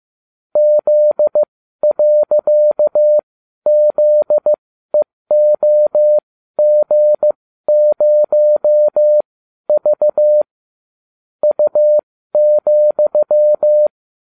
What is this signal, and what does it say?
Morse code, heard as a keyed tone, about 15 wpm, spaced as for 12 wpm, 600 Hz. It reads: Z.ZEOG0V U,